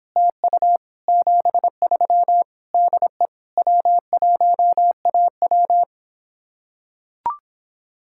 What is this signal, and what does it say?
Morse code, heard as a keyed tone, about 26 wpm, 700 Hz. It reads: TU 73 DE W1AW